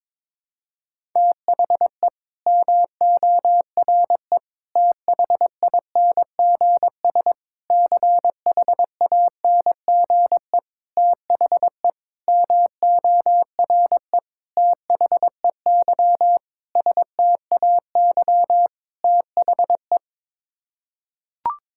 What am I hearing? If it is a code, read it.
Morse code, heard as a keyed tone, about 22 wpm, 700 Hz. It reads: THE MORE THINGS CHANGE THE MORE THEY STAY THE
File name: Sets of 9 Words - 22wpm